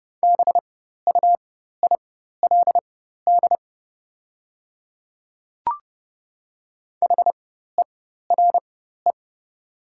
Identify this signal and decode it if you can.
Morse code, heard as a keyed tone, about 29 wpm, spaced as for 14 wpm, 700 Hz. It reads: BUILD HERE
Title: 4x Character Spacing - Top 200 Words - 30wpm